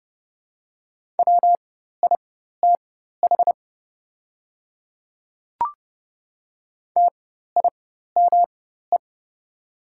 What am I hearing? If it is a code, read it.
Morse code, heard as a keyed tone, about 30 wpm, spaced as for 14 wpm, 700 Hz. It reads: WITH TIME
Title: 4x Character Spacing - Top 200 Words - 30wpm